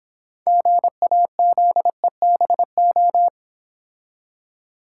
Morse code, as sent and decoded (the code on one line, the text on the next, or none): --. .- --.. -... ---
GAZEBO